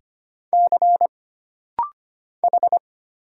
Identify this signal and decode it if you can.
Morse code, heard as a keyed tone, about 25 wpm, 700 Hz. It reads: C H
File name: Single Character (Letter:Number:Punctuation:Prosign) - Rapid-Fire - 25wpm